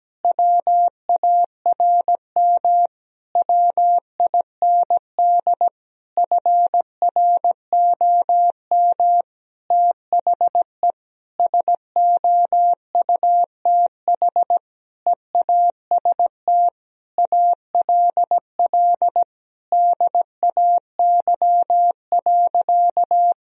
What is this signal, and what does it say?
Morse code, heard as a keyed tone, about 17 wpm, 700 Hz. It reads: WARM WIND FROM THE SOUTH EAST ALL DAY.
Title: Sentences from Top 1000 Words - Review Part 3 - 17wpm